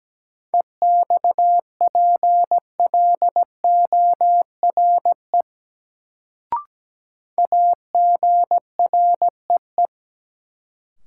Morse code, heard as a keyed tone, about 17 wpm, 700 Hz. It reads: EXPLORE AGREE
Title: Sentences from Top 1100 Words - New Word Drill Alt - 17wpm